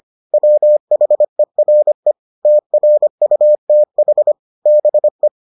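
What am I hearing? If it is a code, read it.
Morse code, heard as a keyed tone, about 25 wpm, 600 Hz. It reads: WHERE TRUTH BE
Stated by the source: Morse edition of the Foundations of Amateur Radio episodes.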